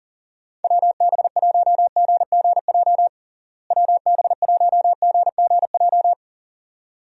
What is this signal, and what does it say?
Morse code, heard as a keyed tone, about 39 wpm, 700 Hz. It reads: WB1GGJ WB1GGJ
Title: USA Call Signs - 6 Characters - Repeated 2x - 40wpm